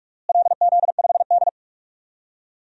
Morse code, heard as a keyed tone, about 46 wpm, 700 Hz. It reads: LZ5D
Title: DXCC-100 Call Signs - 4 Characters - Repeated 2x - 45wpm